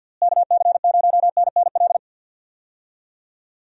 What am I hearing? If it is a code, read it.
Morse code, heard as a keyed tone, about 51 wpm, 700 Hz. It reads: KC0NNL
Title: USA Call Signs - 6 Characters - Repeated 3x - 50wpm